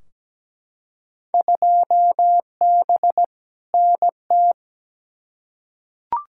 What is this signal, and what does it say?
Morse code, heard as a keyed tone, about 17 wpm, 700 Hz. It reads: 2B NT
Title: Field Day Exchanges Encore - 17wpm